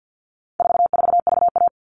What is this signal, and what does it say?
Morse code, heard as a keyed tone, about 50 wpm, 700 Hz. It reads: 44VA
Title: All Cut Numbers - 4-Digit 50wpm